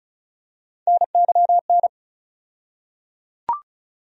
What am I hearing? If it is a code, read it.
Morse code, heard as a keyed tone, about 34 wpm, 700 Hz. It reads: NYN
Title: USA Call Sign Postfixes - 3 Letters 35wpm